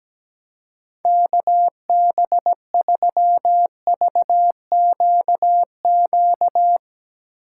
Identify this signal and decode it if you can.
Morse code, heard as a keyed tone, about 17 wpm, 700 Hz. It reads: KB3VQQ
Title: USA Call Signs - 6 Characters - 2x Character Spacing - 17wpm